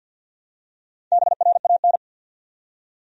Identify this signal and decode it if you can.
Morse code, heard as a keyed tone, about 50 wpm, 700 Hz. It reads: BRAN
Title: USA Common Names - Max 4 Letters - 50wpm